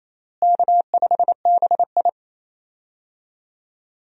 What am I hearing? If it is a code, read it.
Morse code, heard as a keyed tone, about 27 wpm, 700 Hz. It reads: K5BI
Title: USA Call Signs - 4 Characters - Repeated 2x - 28wpm